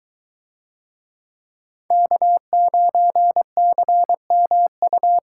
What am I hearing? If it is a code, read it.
Morse code, heard as a keyed tone, about 22 wpm, 700 Hz. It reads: K9CMU